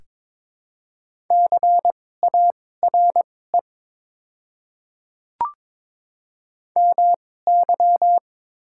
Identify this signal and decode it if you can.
Morse code, heard as a keyed tone, about 22 wpm, spaced as for 16 wpm, 690 Hz. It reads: CARE MY
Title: 2x Character Spacing - Top 500 Words - 22wpm